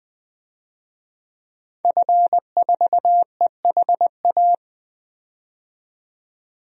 Morse code, heard as a keyed tone, about 20 wpm, 700 Hz. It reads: F4EHA
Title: DXCC-100 Call Signs - 5 Characters - 20wpm